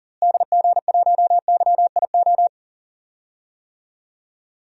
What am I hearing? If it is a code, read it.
Morse code, heard as a keyed tone, about 40 wpm, 700 Hz. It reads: DG1YIO
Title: DXCC-100 Call Signs - 6 Characters - Repeated 2x - 40wpm